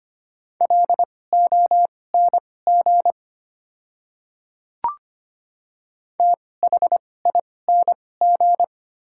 Morse code, heard as a keyed tone, about 25 wpm, spaced as for 18 wpm, 690 Hz. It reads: LONG THING
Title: 2x Character Spacing - Top 100 Words - 25wpm